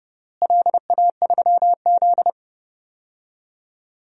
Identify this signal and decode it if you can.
Morse code, heard as a keyed tone, about 32 wpm, 700 Hz. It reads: LA3Z
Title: DXCC-100 Call Signs - 4 Characters - Repeated 3x - 30wpm